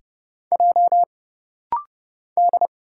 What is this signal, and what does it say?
Morse code, heard as a keyed tone, about 30 wpm, 700 Hz. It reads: J D
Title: Single Letters - Rapid-Fire 30wpm